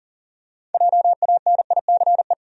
Morse code, heard as a keyed tone, about 37 wpm, 700 Hz. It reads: JANICE
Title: USA Common Names - 40wpm